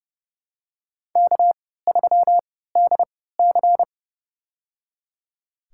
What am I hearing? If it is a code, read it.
Morse code, heard as a keyed tone, about 30 wpm, spaced as for 17 wpm, 700 Hz. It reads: K3DC
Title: USA Call Signs - 3x Character Spacing - 30wpm